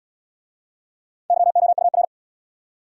Morse code, heard as a keyed tone, about 65 wpm, 700 Hz. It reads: YOUR